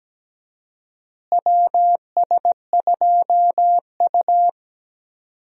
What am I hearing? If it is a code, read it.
Morse code, heard as a keyed tone, about 17 wpm, 700 Hz. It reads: WS2U
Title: USA Call Signs - 4 Characters - Repeated 2x - 17wpm